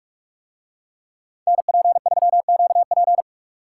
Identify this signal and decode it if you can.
Morse code, heard as a keyed tone, about 44 wpm, 700 Hz. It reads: NP3QP